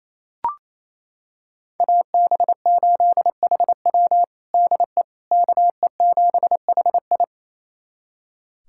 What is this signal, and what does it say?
Morse code, heard as a keyed tone, about 27 wpm, 700 Hz. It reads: AB8HW DE KE7HI
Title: USA Callsign DE Callsign - 5 Characters - 28wpm